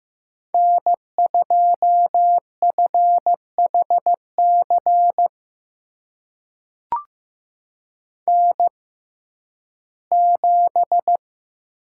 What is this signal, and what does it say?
Morse code, heard as a keyed tone, about 15 wpm, spaced as for 11 wpm, 700 Hz. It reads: N2FHC N7
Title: USA Call Signs - 5 Characters - 6x Character Spacing - 15wpm